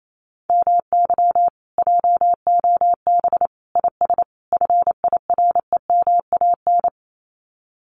Morse code, heard as a keyed tone, about 28 wpm, 700 Hz. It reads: MY JOB IS FIREMAN
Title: Common QSO Phrases Encore 28wpm